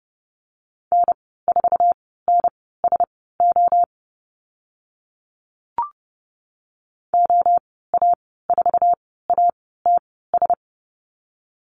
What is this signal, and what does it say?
Morse code, heard as a keyed tone, about 30 wpm, spaced as for 17 wpm, 700 Hz. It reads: N4NSO OA4ATS
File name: Lesson 14g - Number 2 - Call Signs 30wpm